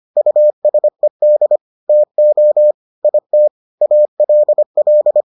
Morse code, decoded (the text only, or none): USED TO IT ALL